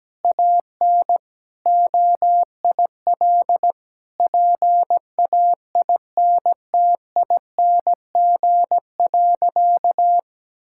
Morse code, heard as a keyed tone, about 17 wpm, 700 Hz. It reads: AN OIL PAINTING.
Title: Sentences from Top 1000 Words - Review Part 6 - 17wpm